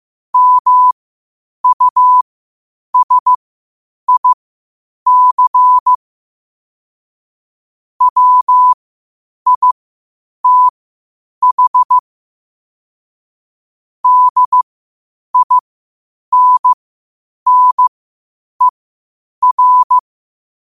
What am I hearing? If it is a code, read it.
Morse code, heard as a keyed tone, about 15 wpm, spaced as for 8 wpm, 1000 Hz. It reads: MUSIC WITH DINNER
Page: Quotes for Sat, 16 Aug 2025 in Morse Code at 8 words per minute.